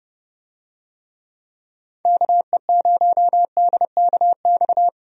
Morse code, heard as a keyed tone, about 31 wpm, 700 Hz. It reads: KE0DKX
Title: USA Call Signs - 6 Characters - 30wpm